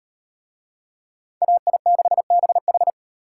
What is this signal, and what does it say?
Morse code, heard as a keyed tone, about 34 wpm, 700 Hz. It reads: AI6BH